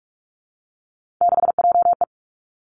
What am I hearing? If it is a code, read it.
Morse code, heard as a keyed tone, about 46 wpm, 700 Hz. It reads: 6JE
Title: Three Challenging Letter-Number Combinations 45wpm